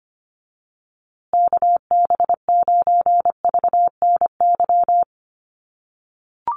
Morse code, heard as a keyed tone, about 25 wpm, 700 Hz. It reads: KB9VNY
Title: USA Call Signs Repeated 3x 25wpm